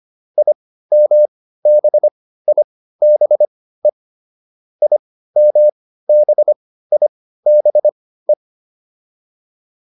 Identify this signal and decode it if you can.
Morse code, heard as a keyed tone, about 25 wpm, spaced as for 15 wpm, 600 Hz. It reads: IMBIBE IMBIBE
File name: IM6 15